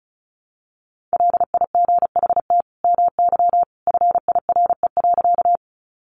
Morse code, heard as a keyed tone, about 34 wpm, 700 Hz. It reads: LIGHT MY FIRE.
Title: Sentences from Top 400 Words 35wpm